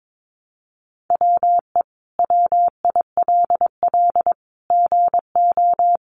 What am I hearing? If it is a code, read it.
Morse code, heard as a keyed tone, about 22 wpm, 700 Hz. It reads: WE WILL GO
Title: Sentences from Top 1300 Words 22wpm